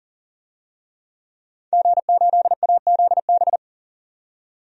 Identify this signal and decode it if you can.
Morse code, heard as a keyed tone, about 40 wpm, 700 Hz. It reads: G8AZB
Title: DXCC-100 Call Signs - 5 Characters - 40wpm